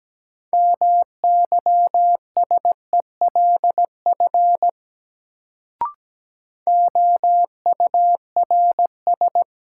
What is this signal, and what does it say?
Morse code, heard as a keyed tone, about 17 wpm, 700 Hz. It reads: MYSELF OURS
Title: Sentences from Top 1300 Words - New Word Drill Alt - 17wpm